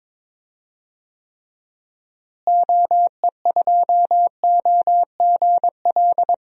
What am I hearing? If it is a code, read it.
Morse code, heard as a keyed tone, about 23 wpm, 700 Hz. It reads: OE2OGL
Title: DXCC-100 Call Signs - 6 Characters - Repeated 2x - 22wpm